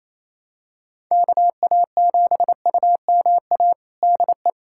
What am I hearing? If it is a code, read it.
Morse code, heard as a keyed tone, about 28 wpm, 700 Hz. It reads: KA7UMA DE
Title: USA Callsign DE Callsign - 6 Characters - 28wpm